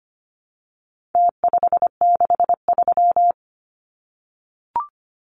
Morse code, heard as a keyed tone, about 25 wpm, 700 Hz. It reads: T563
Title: Numbers - Contest Style - 4-Digit 25wpm